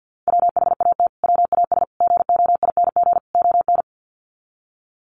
Morse code, heard as a keyed tone, about 50 wpm, 700 Hz. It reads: WHAT WAS GOING ON
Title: Sets of 4 Words 50wpm